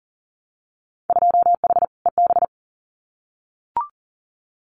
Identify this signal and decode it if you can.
Morse code, heard as a keyed tone, about 37 wpm, 700 Hz. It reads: 2H EB
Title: Winter Field Day Exchanges Encore 40wpm